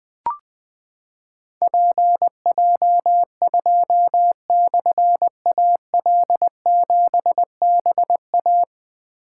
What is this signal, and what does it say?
Morse code, heard as a keyed tone, about 20 wpm, 700 Hz. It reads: PJ2/AL7BA
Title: Difficult Call Signs 20wpm